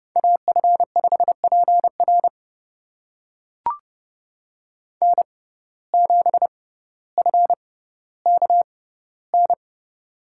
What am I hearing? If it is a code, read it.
Morse code, heard as a keyed tone, about 31 wpm, spaced as for 16 wpm, 700 Hz. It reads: AF5PR N7FKN
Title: USA Call Signs - 5 Characters - 6x Character Spacing - 30wpm